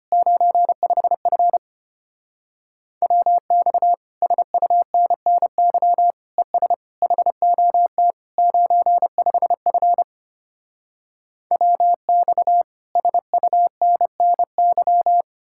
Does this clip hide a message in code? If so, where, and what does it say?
Yes: Morse code from 0.1 to 15.2 s, heard as a keyed tone, about 33 wpm, 700 Hz. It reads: 95F WX SUNNY ES HOT 95F WX SUNNY